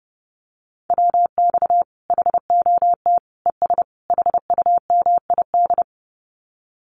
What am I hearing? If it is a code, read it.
Morse code, heard as a keyed tone, about 28 wpm, 700 Hz. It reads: WX HOT ES HUMID
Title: Common QSO Phrases 30wpm